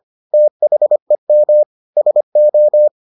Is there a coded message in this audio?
Yes: Morse code from 0.3 to 2.9 s, heard as a keyed tone, about 25 wpm, 600 Hz. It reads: THEM SO